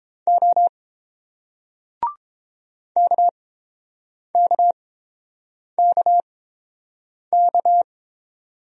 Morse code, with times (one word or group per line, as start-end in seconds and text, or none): 0.3-0.7 O
3.0-7.8 KKKK